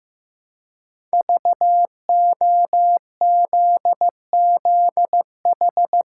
Morse code, heard as a keyed tone, about 15 wpm, 700 Hz. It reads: VOZZH
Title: Code Groups - 5 Letter:Numbers - 15wpm